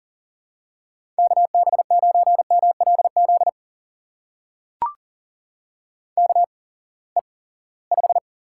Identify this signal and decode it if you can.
Morse code, heard as a keyed tone, about 39 wpm, spaced as for 22 wpm, 700 Hz. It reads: KB9MLZ KE5
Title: USA Call Signs - 6 Characters - 8x Character Spacing - 40wpm